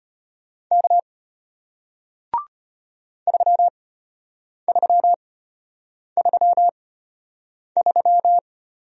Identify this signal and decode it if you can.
Morse code, heard as a keyed tone, about 33 wpm, spaced as for 8 wpm, 700 Hz. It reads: K 3333